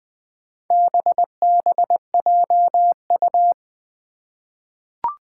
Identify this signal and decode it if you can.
Morse code, heard as a keyed tone, about 20 wpm, 700 Hz. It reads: BBJU
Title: Four Challenging Letter-Number Combinations 20wpm